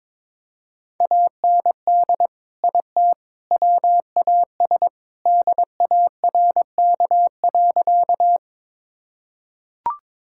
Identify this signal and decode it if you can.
Morse code, heard as a keyed tone, about 22 wpm, 700 Hz. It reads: AND IT WAS DARK.
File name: practice / 22wpm / Sentences with 3-Word Phrases - 22wpm